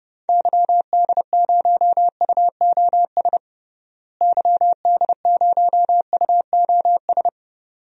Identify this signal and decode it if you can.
Morse code, heard as a keyed tone, about 31 wpm, 700 Hz. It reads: YD0UOS YD0UOS
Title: DXCC-100 Call Signs - 6 Characters - Repeated 2x - 30wpm